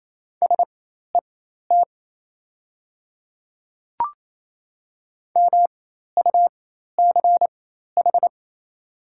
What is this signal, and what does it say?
Morse code, heard as a keyed tone, about 28 wpm, spaced as for 13 wpm, 690 Hz. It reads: SET MUCH
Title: 4x Character Spacing - Top 200 Words - 28wpm